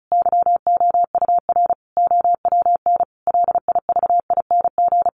Morse code, heard as a keyed tone, about 34 wpm, 700 Hz. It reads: YOUR OWN LIVING
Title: Sentences from Top 1500 Words 35wpm